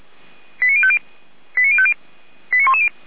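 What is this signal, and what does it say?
Five-tone selective call (ZVEI), heard as quick runs of five tones, tones of about 70 ms. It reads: 89050 89050 89110
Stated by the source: ZVEI five-tone call 89276 received by SDR, from the feuerwehr-orlamuende.org fire brigade server.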